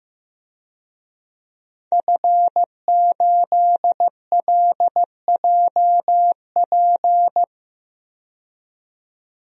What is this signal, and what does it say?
Morse code, heard as a keyed tone, about 15 wpm, 700 Hz. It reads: F8LJP
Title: DXCC-100 Call Signs - 5 Characters - 15wpm